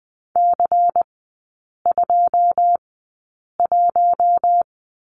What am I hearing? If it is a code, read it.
Morse code, heard as a keyed tone, about 20 wpm, spaced as for 8 wpm, 700 Hz. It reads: C21
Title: 152 Lesson 25e - Letter p - Three Characters 20wpm